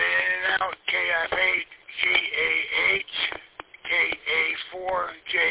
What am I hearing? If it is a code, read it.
Emergency Alert System (SAME) tones, heard as harsh buzzing bursts, with speech. It is heard from 0.8 to 4.5 s.